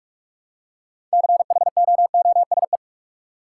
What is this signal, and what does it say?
Morse code, heard as a keyed tone, about 41 wpm, 700 Hz. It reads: CHOOSE
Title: Six Letter Words - 45wpm